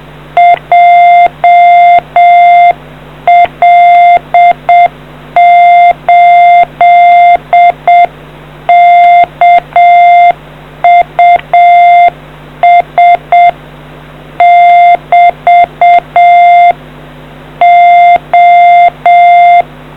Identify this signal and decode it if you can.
Morse code, heard as a keyed tone, about 7 wpm, 2100 Hz. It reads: JL8KUS <BT> O